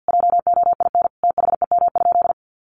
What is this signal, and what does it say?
Morse code, heard as a keyed tone, about 50 wpm, 700 Hz. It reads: JOIN THEM?
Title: Sentences from Top 1400 Words 50wpm